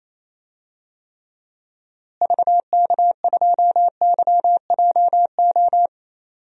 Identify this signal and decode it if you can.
Morse code, heard as a keyed tone, about 28 wpm, 700 Hz. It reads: VK2YJO
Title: DXCC-100 Call Signs - 6 Characters - Repeated 2x - 28wpm